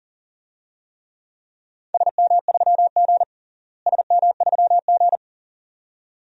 Morse code, heard as a keyed tone, about 40 wpm, 700 Hz. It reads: SM3G SM3G
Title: DXCC-100 Call Signs - 4 Characters - Repeated 2x - 40wpm